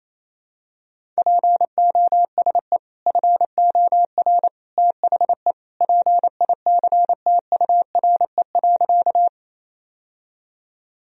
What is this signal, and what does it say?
Morse code, heard as a keyed tone, about 28 wpm, 700 Hz. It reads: POSE FOR THE PICTURE.
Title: Sentences from Top 1000 Words - Review Part 2 - 28wpm